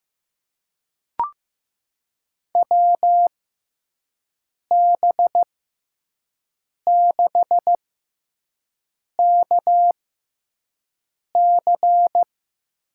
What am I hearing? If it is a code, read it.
Morse code, heard as a keyed tone, about 15 wpm, spaced as for 5 wpm, 700 Hz. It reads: WB6KC
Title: USA Call Signs - 6 Characters - 6x Character Spacing - 15wpm